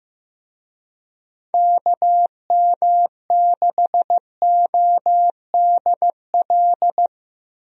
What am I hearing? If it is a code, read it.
Morse code, heard as a keyed tone, about 15 wpm, 700 Hz. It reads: KM6ODL
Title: USA Call Signs - 6 Characters - Repeated 2x - 15wpm